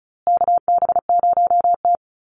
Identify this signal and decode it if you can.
Morse code, heard as a keyed tone, about 34 wpm, 700 Hz. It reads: KB0T
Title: USA Call Signs Repeated 3x 35wpm